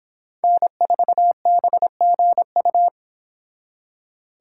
Morse code, heard as a keyed tone, about 25 wpm, 700 Hz. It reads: N4BGU